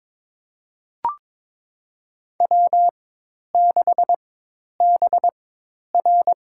Morse code, heard as a keyed tone, about 22 wpm, spaced as for 10 wpm, 700 Hz. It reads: W6BR
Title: USA Call Signs - 4 Characters - 4x Character Spacing - 22wpm